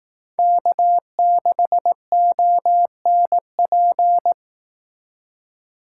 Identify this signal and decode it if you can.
Morse code, heard as a keyed tone, about 18 wpm, 700 Hz. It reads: K6ONP
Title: Speed Racing - USA 1x3 Call Signs 15wpm